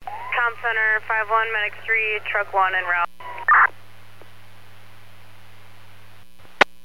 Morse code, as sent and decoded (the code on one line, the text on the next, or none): none